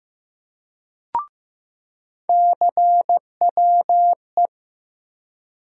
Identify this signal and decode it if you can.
Morse code, heard as a keyed tone, about 15 wpm, 700 Hz. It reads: CWE
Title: USA Call Sign Postfixes - 3 Letters 15wpm